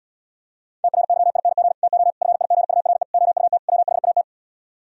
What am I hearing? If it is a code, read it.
Morse code, heard as a keyed tone, about 62 wpm, 700 Hz. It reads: TRYING TO FIGURE OUT WHAT
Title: Speed-Racing - Sets of 5 Words 50wpm